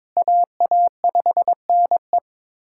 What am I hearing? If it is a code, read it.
Morse code, heard as a keyed tone, about 22 wpm, 700 Hz. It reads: AA5NE
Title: USA Callsign DE Callsign - 5 Characters - 22wpm